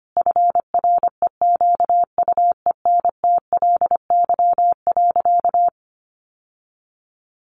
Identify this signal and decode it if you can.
Morse code, heard as a keyed tone, about 25 wpm, 700 Hz. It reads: FREQUENTLY.
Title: Sentences from Top 1300 Words 25wpm